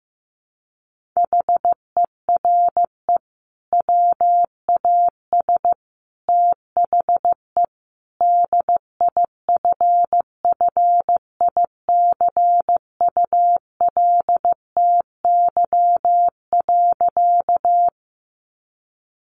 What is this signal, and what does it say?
Morse code, heard as a keyed tone, about 15 wpm, 700 Hz. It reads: HERE WAS THE DIFFICULTY.
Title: Sentences from Top 1500 Words 15wpm